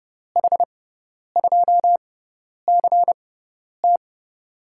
Morse code, heard as a keyed tone, about 30 wpm, spaced as for 10 wpm, 700 Hz. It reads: H2CT
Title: USA Call Signs - 5 Characters - 6x Character Spacing - 30wpm